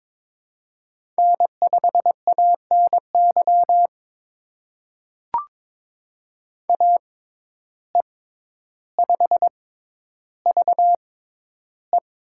Morse code, heard as a keyed tone, about 22 wpm, spaced as for 11 wpm, 700 Hz. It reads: N5ANY AE5VE
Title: USA Call Signs - 5 Characters - 6x Character Spacing - 22wpm